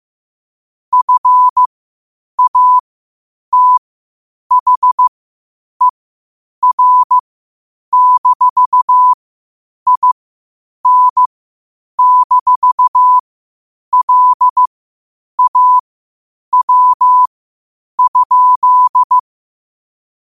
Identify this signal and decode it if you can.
Morse code, heard as a keyed tone, about 15 wpm, spaced as for 8 wpm, 1000 Hz. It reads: FATHER-IN-LAW?
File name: qotdfiles / 8wpm